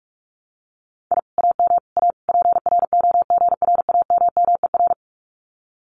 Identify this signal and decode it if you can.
Morse code, heard as a keyed tone, about 44 wpm, 700 Hz. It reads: I AM A PROGRAMMER
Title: Common QSO Phrases Encore 45wpm